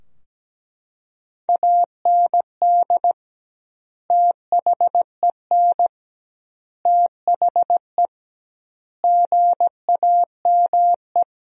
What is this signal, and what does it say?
Morse code, heard as a keyed tone, about 17 wpm, 700 Hz. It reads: AND THEN THE GAME
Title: 2x Word-Spacing - Sentences from Top 500 Words - 17wpm